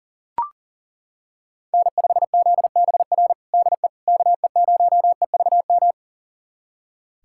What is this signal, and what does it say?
Morse code, heard as a keyed tone, about 40 wpm, 700 Hz. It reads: N5ZBR DE KE0EVM
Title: USA Callsign DE Callsign 40wpm